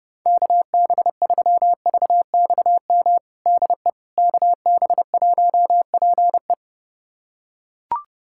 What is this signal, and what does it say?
Morse code, heard as a keyed tone, about 30 wpm, 700 Hz. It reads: KB3VXM DE KB1PE
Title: USA Callsign DE Callsign 30wpm